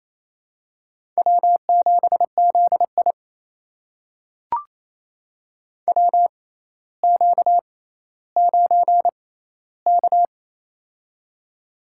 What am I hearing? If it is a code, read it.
Morse code, heard as a keyed tone, about 28 wpm, spaced as for 14 wpm, 700 Hz. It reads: W7ZI WQ9K
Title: USA Call Signs - 4 Characters - 6x Character Spacing - 28wpm